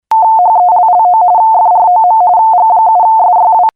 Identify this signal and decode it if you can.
RTTY, about 45.45 baud. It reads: HELLO FROM G7SYW